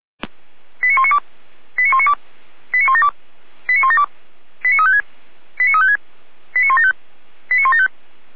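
Five-tone selective call (ZVEI), heard as quick runs of five tones, tones of about 70 ms. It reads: 89192 89192 89172 89172 89356 89356 89175 89175
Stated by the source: ZVEI five-tone call received by SDR, from feuerwehr-orlamuende.org (fire brigade Orlamünde).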